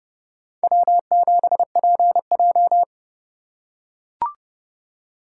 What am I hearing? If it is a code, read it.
Morse code, heard as a keyed tone, about 30 wpm, 700 Hz. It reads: W7PJ
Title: USA Call Signs - 4 Characters - 3x Character Spacing - 30wpm